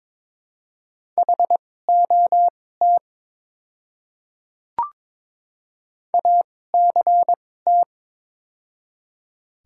Morse code, heard as a keyed tone, about 22 wpm, spaced as for 16 wpm, 700 Hz. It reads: HOT ACT
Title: 2x Character Spacing - Top 200 Words - 22wpm